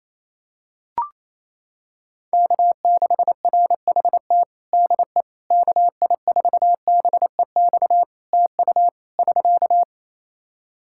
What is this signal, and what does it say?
Morse code, heard as a keyed tone, about 28 wpm, 700 Hz. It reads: K6RHT DE KI4BEX TU <SK>
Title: K1USN SST QSO Practice 28wpm